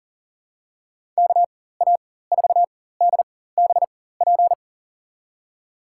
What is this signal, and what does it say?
Morse code, heard as a keyed tone, about 40 wpm, spaced as for 19 wpm, 700 Hz. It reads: KA4DBP
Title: USA Call Signs - 6 Characters - 4x Character Spacing - 40wpm